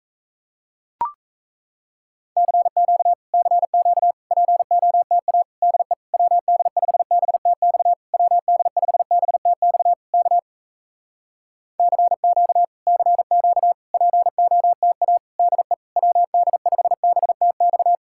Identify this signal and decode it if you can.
Morse code, heard as a keyed tone, about 38 wpm, 700 Hz. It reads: CQ CQ POTA DE WD5BTX WD5BTX K CQ CQ POTA DE WD5BTX